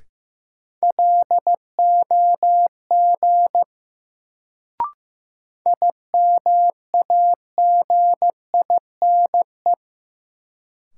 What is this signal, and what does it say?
Morse code, heard as a keyed tone, about 15 wpm, 700 Hz. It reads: LOG IMAGINE
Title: Sentences from Top 1400 Words - New Word Drill - 15wpm